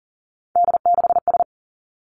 Morse code, heard as a keyed tone, about 42 wpm, 700 Hz. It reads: D6S
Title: Three Challenging Letter-Number Combinations 40wpm